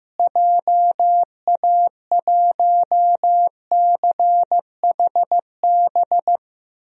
Morse code, heard as a keyed tone, about 15 wpm, 700 Hz. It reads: JA1CHB
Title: DXCC-100 Call Signs - 6 Characters - Repeated 3x - 15wpm